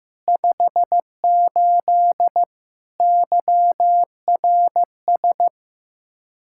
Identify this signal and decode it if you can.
Morse code, heard as a keyed tone, about 15 wpm, 700 Hz. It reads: 58 YRS
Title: Practice QSO - Exchange 3 - More Info - 15wpm